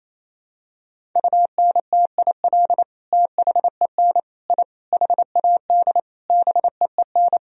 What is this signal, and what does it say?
Morse code, heard as a keyed tone, about 28 wpm, 700 Hz. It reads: UNTIL THEN I HAD BEEN
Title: Sentences from Top 1000 Words - Review Part 1 - 28wpm